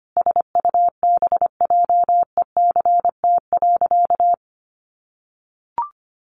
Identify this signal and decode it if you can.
Morse code, heard as a keyed tone, about 25 wpm, 700 Hz. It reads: SUBJECT.
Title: Sentences from Top 800 Words 25wpm